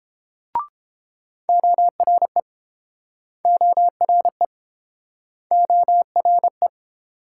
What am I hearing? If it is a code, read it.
Morse code, heard as a keyed tone, about 30 wpm, 700 Hz. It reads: ORE ORE ORE